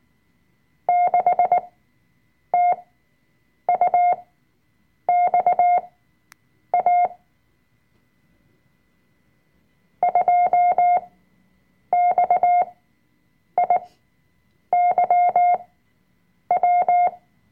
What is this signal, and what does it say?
Morse code, heard as a keyed tone, about 19 wpm, spaced as for 7 wpm, 710 Hz. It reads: 6TUXA 2XIYW
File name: Workshop Dateien / CW-Online-Kurs / Lektion 005